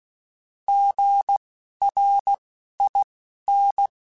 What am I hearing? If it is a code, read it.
Morse code, heard as a keyed tone, about 16 wpm, spaced as for 11 wpm, 800 Hz. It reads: GRIN
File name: Alphabetical / Alphabetical mp3s / Gs @10